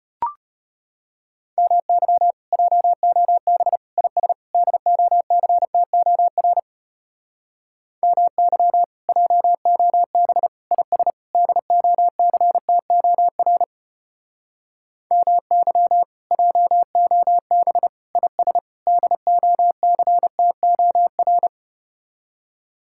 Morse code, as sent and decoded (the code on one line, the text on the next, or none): -- -.-- / .--- --- -... / .. ... / -.. --- -.-. - --- .-. / -- -.-- / .--- --- -... / .. ... / -.. --- -.-. - --- .-. / -- -.-- / .--- --- -... / .. ... / -.. --- -.-. - --- .-.
MY JOB IS DOCTOR MY JOB IS DOCTOR MY JOB IS DOCTOR